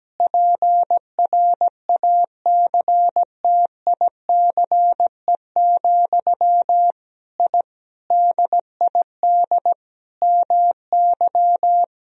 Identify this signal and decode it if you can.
Morse code, heard as a keyed tone, about 17 wpm, 700 Hz. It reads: PRACTICE, I DID MY
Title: Sentences from Top 1000 Words - Review Part 4 - 17wpm